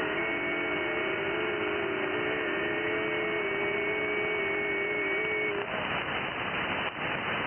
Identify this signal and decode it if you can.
FT8.
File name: IC-7100